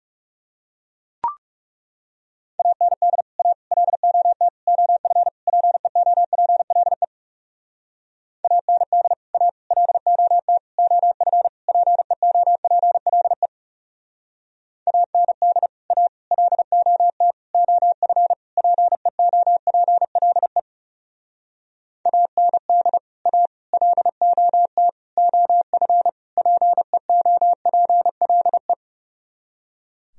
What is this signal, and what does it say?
Morse code, heard as a keyed tone, about 35 wpm, 700 Hz. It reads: AND A LOT OF PEOPLE AND A LOT OF PEOPLE AND A LOT OF PEOPLE AND A LOT OF PEOPLE
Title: Speed-Racing - Sets of 5 Words 30wpm